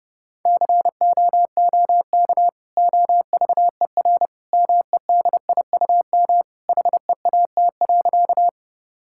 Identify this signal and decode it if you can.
Morse code, heard as a keyed tone, about 30 wpm, 700 Hz. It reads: COOK OVER MEDIUM HEAT.
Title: Sentences from Top 1000 Words - Review Part 6 - 30wpm